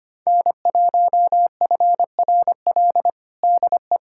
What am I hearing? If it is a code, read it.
Morse code, heard as a keyed tone, about 25 wpm, 700 Hz. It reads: N1FRL DE
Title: USA Callsign DE Callsign 25wpm